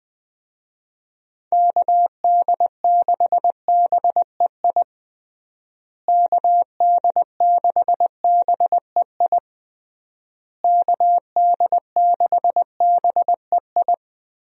Morse code, heard as a keyed tone, about 20 wpm, 700 Hz. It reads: KD6BEI KD6BEI KD6BEI
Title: USA Call Signs - 6 Characters - Repeated 3x - 20wpm